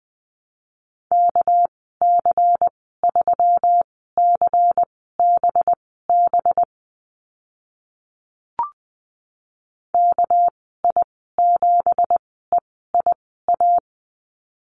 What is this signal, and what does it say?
Morse code, heard as a keyed tone, about 20 wpm, spaced as for 14 wpm, 700 Hz. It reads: KC3CBB KI7EIA
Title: 207 Lesson 33g - Letter j - Call Signs 20wpm